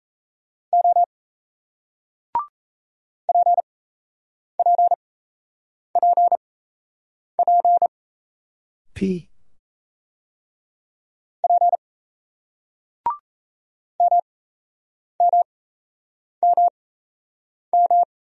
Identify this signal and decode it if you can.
Morse code, heard as a keyed tone, about 38 wpm, spaced as for 8 wpm, 700 Hz. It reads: O PPPP P MMMM